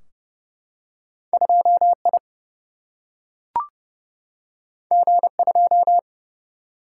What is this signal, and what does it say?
Morse code, heard as a keyed tone, about 30 wpm, 700 Hz. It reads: 2I G2
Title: Single-Letter Single-Number 30wpm